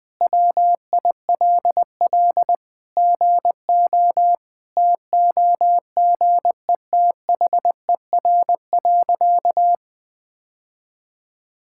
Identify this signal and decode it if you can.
Morse code, heard as a keyed tone, about 20 wpm, 700 Hz. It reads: WILL GO TOGETHER.
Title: Sentences with 2-Word Phrases - 20wpm